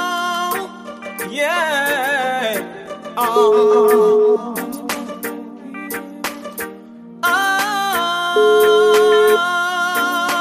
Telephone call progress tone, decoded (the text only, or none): ringback tone